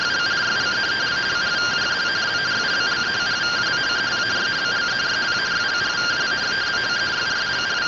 Two-tone teleprinter signal, about 170 Hz shift.
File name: findings / AXIAL fr1 3mm.gen-30 rtty rx